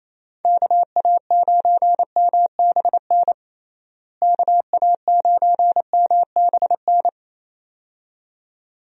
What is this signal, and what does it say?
Morse code, heard as a keyed tone, about 27 wpm, 700 Hz. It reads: KA9MBN KA9MBN
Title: USA Call Signs - 6 Characters - Repeated 2x - 28wpm